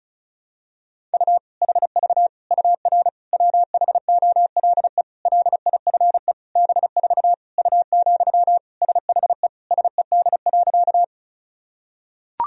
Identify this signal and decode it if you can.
Morse code, heard as a keyed tone, about 34 wpm, 700 Hz. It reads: U HV UR WHOLE LIFE B4 U, SHE SED.
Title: Sentences from Top 400 Words and Abbreviations 35wpm